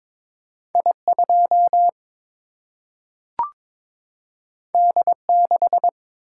Morse code, heard as a keyed tone, about 22 wpm, 700 Hz. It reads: I 2 D 6
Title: Single-Letter Single-Number 22wpm